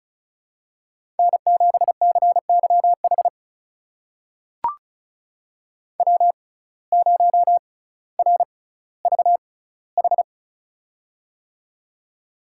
Morse code, heard as a keyed tone, about 36 wpm, spaced as for 18 wpm, 700 Hz. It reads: N7CYH W0RVH
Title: USA Call Signs - 5 Characters - 6x Character Spacing - 35wpm